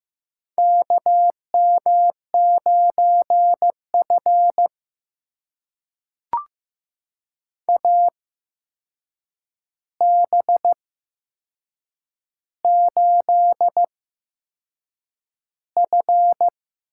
Morse code, heard as a keyed tone, about 15 wpm, spaced as for 6 wpm, 700 Hz. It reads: KM9F AB8F
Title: USA Call Signs - 4 Characters - 8x Character Spacing - 15wpm